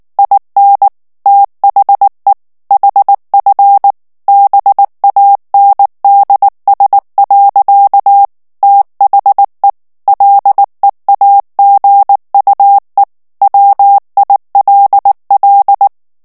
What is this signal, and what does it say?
Morse code, heard as a keyed tone, about 19 wpm, 800 Hz. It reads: IN THE HF BANDS. THE LEAGUE WILL
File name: ANGLAIS texte clair 3 vitesse 18 mots minute